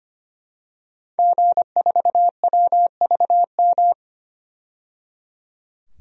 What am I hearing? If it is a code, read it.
Morse code, heard as a keyed tone, about 25 wpm, 700 Hz. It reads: G4WVM
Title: DXCC-100 Call Signs - 5 Characters - 25wpm